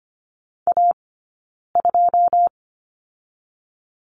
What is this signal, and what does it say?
Morse code, heard as a keyed tone, about 25 wpm, 700 Hz. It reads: A 2
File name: Lesson 15d - Number 5 - Two Characters 25wpm